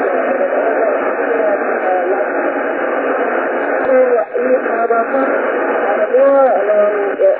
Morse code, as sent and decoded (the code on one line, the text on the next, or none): none